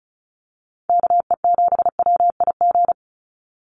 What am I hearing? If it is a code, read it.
Morse code, heard as a keyed tone, about 34 wpm, 700 Hz. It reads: KE7WIG
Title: USA Call Signs 35wpm